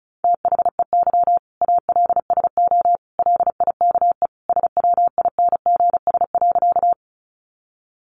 Morse code, heard as a keyed tone, about 34 wpm, 700 Hz. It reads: THEY ALSO LIKE SWINGS.
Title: Easy Phrases 35wpm